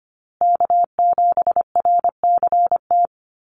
Morse code, heard as a keyed tone, about 25 wpm, 700 Hz. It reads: K7RCT
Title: USA Call Signs Repeated 3x 25wpm